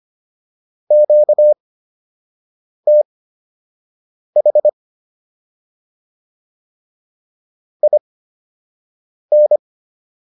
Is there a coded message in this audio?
Yes: Morse code from 0.9 to 9.6 s, heard as a keyed tone, about 25 wpm, spaced as for 6 wpm, 600 Hz. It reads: QTH IN